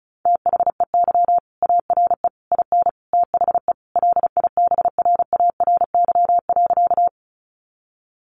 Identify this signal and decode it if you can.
Morse code, heard as a keyed tone, about 34 wpm, 700 Hz. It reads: THEY ARE IN THE LIBRARY.
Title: Sentences from Top 600 Words 35wpm